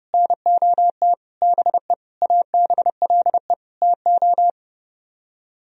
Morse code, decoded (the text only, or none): NOT BE ABLE TO